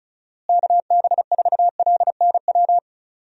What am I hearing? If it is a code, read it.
Morse code, heard as a keyed tone, about 35 wpm, 700 Hz. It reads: KB4LNW
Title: USA Call Signs - 6 Characters - Repeated 2x - 35wpm